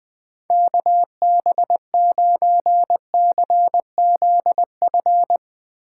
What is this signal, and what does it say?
Morse code, heard as a keyed tone, about 20 wpm, 700 Hz. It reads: KB9CZF
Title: Speed Racing - USA 2x3 Call Signs 17wpm